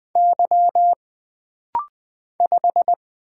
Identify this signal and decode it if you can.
Morse code, heard as a keyed tone, about 20 wpm, 700 Hz. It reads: Y 5
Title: Single Letter-Number - Rapid-Fire 20wpm